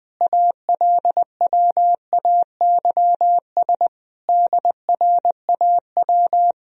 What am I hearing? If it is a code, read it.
Morse code, heard as a keyed tone, about 21 wpm, 700 Hz. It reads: ALWAYS DRAW
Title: Sentences from Top 1000 Words - Review Part 1 - 20wpm